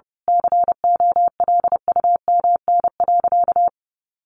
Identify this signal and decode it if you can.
Morse code, heard as a keyed tone, about 30 wpm, 700 Hz. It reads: COLUMN.
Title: A Sound Of Thunder 30wpm